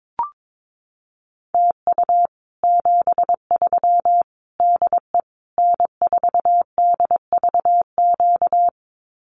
Morse code, heard as a keyed tone, about 22 wpm, 700 Hz. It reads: TU 73 DE N4DVQ
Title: Common QSO Phrases Encore 22wpm